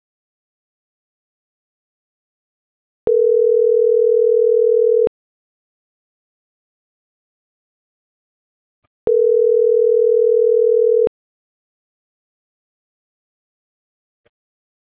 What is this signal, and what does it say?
Ringback tone (North America), 440 + 480 Hz.